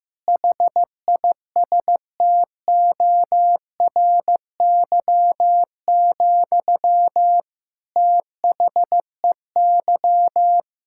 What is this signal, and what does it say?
Morse code, heard as a keyed tone, about 16 wpm, 700 Hz. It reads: HISTORY, THEY
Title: Sentences from Top 1000 Words - Review Part 5 - 15wpm